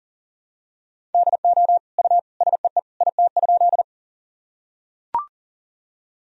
Morse code, heard as a keyed tone, about 40 wpm, 700 Hz. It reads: DO U SEE IT?
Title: Sentences from Top 100 Words and Abbreviations 40wpm